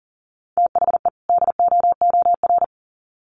Morse code, heard as a keyed tone, about 40 wpm, 700 Hz. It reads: THE DOOR
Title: Sets of 2 Words 40wpm